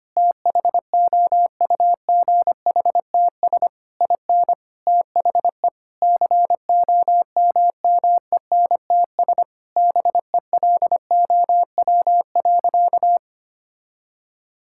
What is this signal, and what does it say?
Morse code, heard as a keyed tone, about 25 wpm, 700 Hz. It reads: THOUGHTS IN THE COMMENTS BELOW.
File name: practice / 25wpm / Sentences with 9-Word Phrases - 25wpm